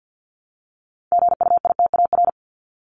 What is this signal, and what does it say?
Morse code, heard as a keyed tone, about 50 wpm, 700 Hz. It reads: GUITAR
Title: Top 1501-1750 Words 50wpm